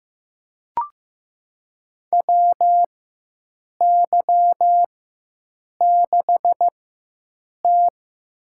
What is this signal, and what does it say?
Morse code, heard as a keyed tone, about 15 wpm, spaced as for 7 wpm, 700 Hz. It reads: WY6T
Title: USA Call Signs - 4 Characters - 4x Character Spacing - 15wpm